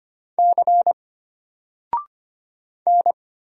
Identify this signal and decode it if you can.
Morse code, heard as a keyed tone, about 25 wpm, 700 Hz. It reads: C N
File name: Single-Letter-Number-25wpm